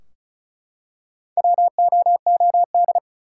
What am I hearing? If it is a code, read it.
Morse code, heard as a keyed tone, about 36 wpm, 700 Hz. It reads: WOOD
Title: Four Letter Words - 35wpm